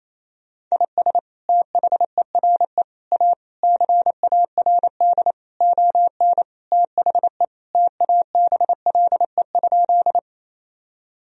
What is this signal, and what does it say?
Morse code, heard as a keyed tone, about 27 wpm, 700 Hz. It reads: IS THERE A CARD ON THE TABLE?
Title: Sentences from Top 1000 Words - Review Part 4 - 28wpm